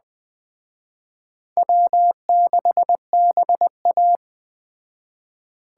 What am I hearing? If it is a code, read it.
Morse code, heard as a keyed tone, about 20 wpm, 700 Hz. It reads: W6BA